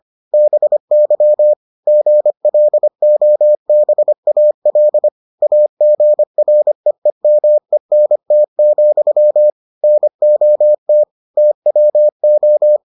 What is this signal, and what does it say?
Morse code, heard as a keyed tone, about 25 wpm, 600 Hz. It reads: BY GLOBAL AGREEMENT, NOT TWO